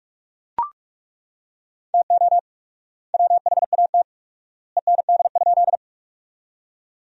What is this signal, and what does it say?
Morse code, heard as a keyed tone, about 41 wpm, 700 Hz. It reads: TO WHAT END?
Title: 4x Word-Spacing - Sentences from Top 200 Words - 45wpm